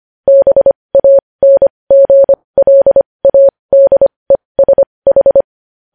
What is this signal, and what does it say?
Morse code, heard as a keyed tone, about 25 wpm, spaced as for 20 wpm, 550 Hz. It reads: BANGLADESH
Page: News Headlines in Morse Code at 20 WPM